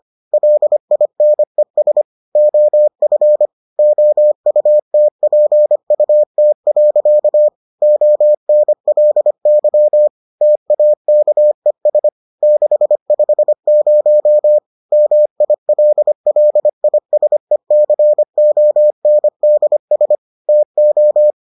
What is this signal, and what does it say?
Morse code, heard as a keyed tone, about 25 wpm, 600 Hz. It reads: LINES OF OUTPUT. ONLY TAKES 650 MILLISECONDS TO